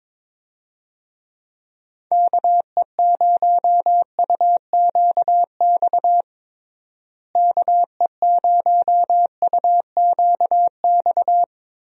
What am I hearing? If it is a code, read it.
Morse code, heard as a keyed tone, about 22 wpm, 700 Hz. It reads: KE0UQX KE0UQX
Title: USA Call Signs - 6 Characters - Repeated 3x - 22wpm